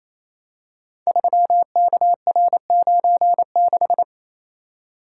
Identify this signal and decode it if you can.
Morse code, heard as a keyed tone, about 27 wpm, 700 Hz. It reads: 3KR96
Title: Code Groups - 5 Letter:Numbers - 28wpm